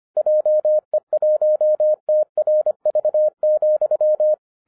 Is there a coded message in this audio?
Yes: Morse code from 0.2 to 4.3 s, heard as a keyed tone, about 25 wpm, 610 Hz. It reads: JE1TRV,